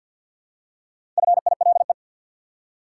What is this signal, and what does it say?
Morse code, heard as a keyed tone, about 51 wpm, 700 Hz. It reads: FILE
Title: Four Letter Words - 50wpm